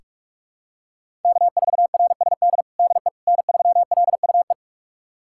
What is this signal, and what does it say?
Morse code, heard as a keyed tone, about 46 wpm, 700 Hz. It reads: K4RSD DE N3LUE
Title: USA Callsign DE Callsign - 5 Characters - 45wpm